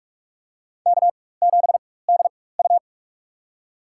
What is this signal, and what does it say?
Morse code, heard as a keyed tone, about 44 wpm, spaced as for 21 wpm, 700 Hz. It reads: K7DU
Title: USA Call Signs - 4 Characters - 4x Character Spacing - 45wpm